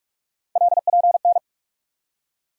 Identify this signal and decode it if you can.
Morse code, heard as a keyed tone, about 42 wpm, 700 Hz. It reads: LPN
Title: K1USN SST QSO Practice 45wpm